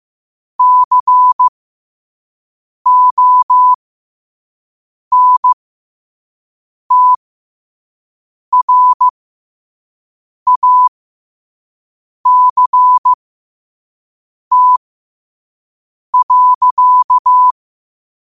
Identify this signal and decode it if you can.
Morse code, heard as a keyed tone, about 15 wpm, spaced as for 5 wpm, 1000 Hz. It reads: CONTRACT.
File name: qotdfiles / 5wpm